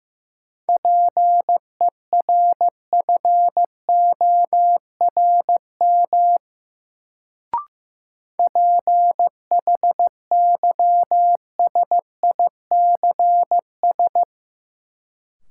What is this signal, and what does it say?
Morse code, heard as a keyed tone, about 15 wpm, 700 Hz. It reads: PERFORM PHYSICS
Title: Sentences from Top 1000 Words - Review Part 5 - New Word Drill - 15wpm